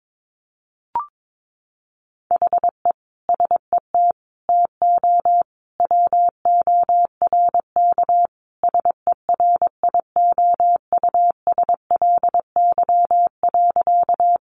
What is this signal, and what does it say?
Morse code, heard as a keyed tone, about 22 wpm, 700 Hz. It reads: HE SET TO WORK SERIOUSLY.
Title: Sentences from Top 1200 Words 22wpm